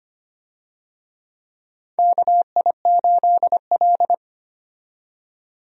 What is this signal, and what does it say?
Morse code, heard as a keyed tone, about 25 wpm, 700 Hz. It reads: KI8L
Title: USA Call Signs - 4 Characters - 25wpm